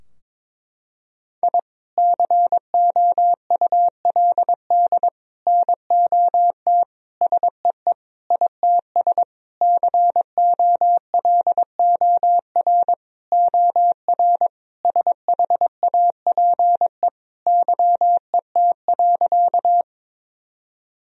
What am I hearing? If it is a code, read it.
Morse code, heard as a keyed tone, about 22 wpm, 700 Hz. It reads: I COULD NOT SEE ITS COLOR OR SHAPE YET.
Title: Sentences from Top 1000 Words - Review Part 3 - 22wpm